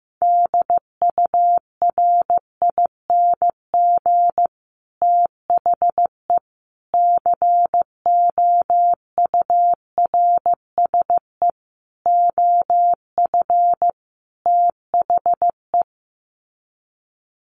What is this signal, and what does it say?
Morse code, heard as a keyed tone, about 15 wpm, 700 Hz. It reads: DURING THE COURSE OF THE
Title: Sets of 5 Words Encore 15wpm